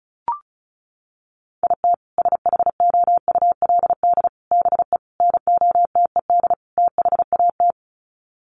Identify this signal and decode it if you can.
Morse code, heard as a keyed tone, about 34 wpm, 700 Hz. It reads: IT SHOULD BE NOTED THAT
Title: Sets of 5 Words 35wpm